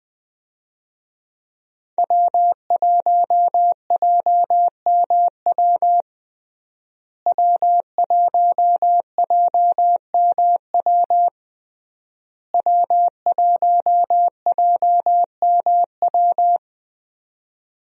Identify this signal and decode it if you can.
Morse code, heard as a keyed tone, about 21 wpm, 700 Hz. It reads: W1JMW W1JMW W1JMW
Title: USA Call Signs - 5 Characters - Repeated 3x - 20wpm